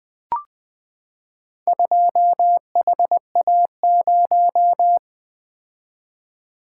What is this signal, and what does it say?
Morse code, heard as a keyed tone, about 20 wpm, 700 Hz. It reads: 2HA0
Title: Four Letter-Number Combinations - 20wpm